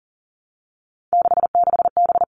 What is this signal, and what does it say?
Morse code, heard as a keyed tone, about 38 wpm, 700 Hz. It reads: 66B
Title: Three Challenging Letter-Number Combinations 40wpm